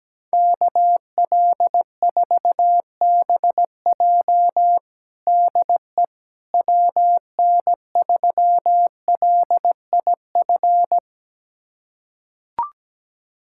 Morse code, heard as a keyed tone, about 17 wpm, 700 Hz. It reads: KL4BJ DE WN3LIF
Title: USA Callsign DE Callsign 17wpm